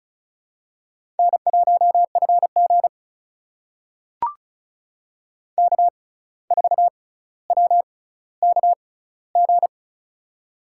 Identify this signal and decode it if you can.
Morse code, heard as a keyed tone, about 36 wpm, spaced as for 17 wpm, 700 Hz. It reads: N1FG K4WKG
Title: USA Call Signs - 6x Character Spacing - 35wpm